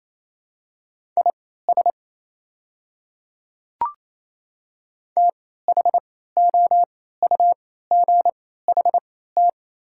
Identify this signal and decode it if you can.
Morse code, heard as a keyed tone, about 27 wpm, spaced as for 16 wpm, 700 Hz. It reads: IS THOUGHT
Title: 3x Character Spacing - Top 500 Words - 28wpm